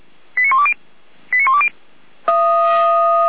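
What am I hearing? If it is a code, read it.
Five-tone selective call (ZVEI), heard as quick runs of five tones, tones of about 70 ms. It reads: 89120 89120
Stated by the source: ZVEI five-tone selective call received on an SDR.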